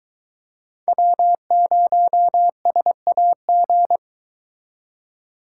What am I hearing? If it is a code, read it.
Morse code, heard as a keyed tone, about 24 wpm, 700 Hz. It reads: W0SAG